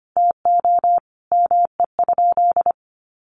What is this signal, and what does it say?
Morse code, heard as a keyed tone, about 25 wpm, 700 Hz. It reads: TO ME?
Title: Sentences from Top 900 Words 25wpm